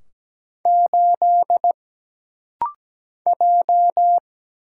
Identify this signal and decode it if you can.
Morse code, heard as a keyed tone, about 17 wpm, 700 Hz. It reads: 8 J
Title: Single Letter-Number - Rapid-Fire 17wpm